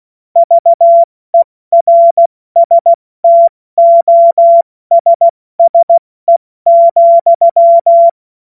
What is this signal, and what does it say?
Morse code, heard as a keyed tone, about 16 wpm, 680 Hz. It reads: VERSTOSSE,
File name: Morsekurs / Witze / Witz 25. Oktober 2022 70 F80